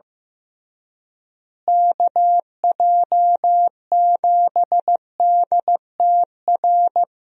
Morse code, heard as a keyed tone, about 15 wpm, 700 Hz. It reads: KJ7DTR
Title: USA Call Signs - 6 Characters - Repeated 3x - 15wpm